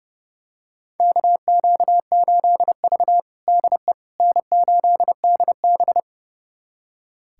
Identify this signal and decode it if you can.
Morse code, heard as a keyed tone, about 30 wpm, 700 Hz. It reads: KQ8V DE N8DB
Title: USA Callsign DE Callsign - 4 Characters - 30wpm